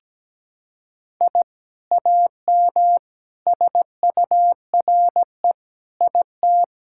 Morse code, heard as a keyed tone, about 17 wpm, 700 Hz. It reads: I AM SURE IT
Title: Sentences from Top 1000 Words - Review Part 5 - 17wpm